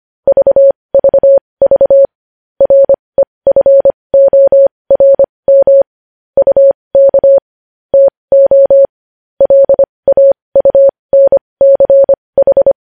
Morse code, heard as a keyed tone, about 25 wpm, spaced as for 20 wpm, 550 Hz. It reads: VVV REFORM UK TO LAUNCH
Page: news-headlines-in-morse-code-at-20-wpm